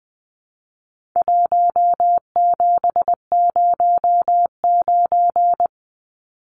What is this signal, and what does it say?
Morse code, heard as a keyed tone, about 20 wpm, 700 Hz. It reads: 1709
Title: Numbers - Contest Style - 4-Digit 20wpm